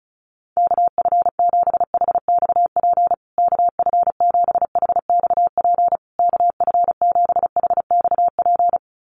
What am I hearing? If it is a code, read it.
Morse code, heard as a keyed tone, about 34 wpm, 700 Hz. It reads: KF7HXP KF7HXP KF7HXP
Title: USA Call Signs Repeated 3x 35wpm